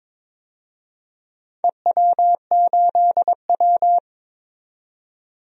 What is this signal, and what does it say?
Morse code, heard as a keyed tone, about 21 wpm, 700 Hz. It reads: EW8W
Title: DXCC-100 Call Signs - 4 Characters - 22wpm